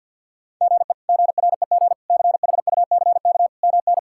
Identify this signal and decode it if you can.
Morse code, heard as a keyed tone, about 51 wpm, 700 Hz. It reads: GE GREG CHUCK MN
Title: K1USN SST QSO Practice - 50wpm